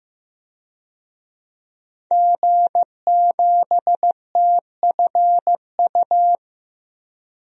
Morse code, heard as a keyed tone, about 15 wpm, 700 Hz. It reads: G7TFU
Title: DXCC-100 Call Signs - 5 Characters - 15wpm